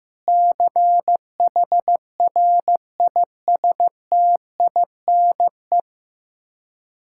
Morse code, heard as a keyed tone, about 15 wpm, 700 Hz. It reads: CHRISTINE
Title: Practice QSO - Exchange 1 - Beginning - 15wpm